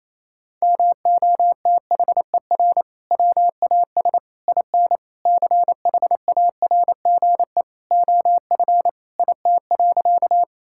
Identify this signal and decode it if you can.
Morse code, heard as a keyed tone, about 28 wpm, 700 Hz. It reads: MOTHER WAS IN CHARGE OF IT.
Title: Sentences from Top 1000 Words - Review Part 6 - 28wpm